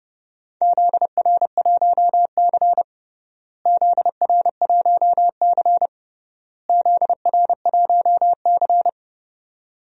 Morse code, heard as a keyed tone, about 30 wpm, 700 Hz. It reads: ZR1C ZR1C ZR1C
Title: DXCC-100 Call Signs - 4 Characters - Repeated 3x - 30wpm